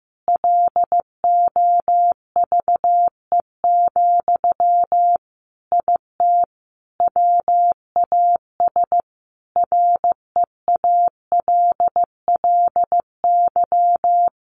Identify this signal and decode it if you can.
Morse code, heard as a keyed tone, about 15 wpm, 700 Hz. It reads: LOVE, IT WAS REALLY
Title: Sentences from Top 1400 Words 15wpm